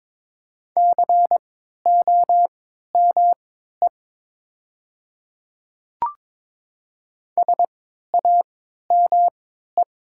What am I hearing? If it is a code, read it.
Morse code, heard as a keyed tone, about 22 wpm, spaced as for 12 wpm, 700 Hz. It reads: COME SAME
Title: 3x Character Spacing - Top 200 Words - 22wpm